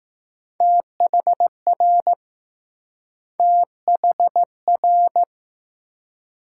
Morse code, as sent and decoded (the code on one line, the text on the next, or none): - .... .-. / - .... .-.
THR THR